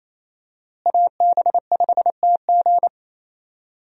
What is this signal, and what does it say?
Morse code, heard as a keyed tone, about 27 wpm, 700 Hz. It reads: AB5TG
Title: USA Call Signs - 5 Characters - Repeated 3x - 28wpm